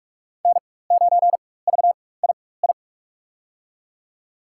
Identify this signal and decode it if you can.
Morse code, heard as a keyed tone, about 46 wpm, spaced as for 21 wpm, 700 Hz. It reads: N9VII
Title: USA Call Signs - 4x Character Spacing - 45wpm